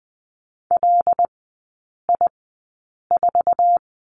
Morse code, heard as a keyed tone, about 20 wpm, spaced as for 8 wpm, 700 Hz. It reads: LI4
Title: 133 Lesson 22e - Question Mark - Three Characters 20wpm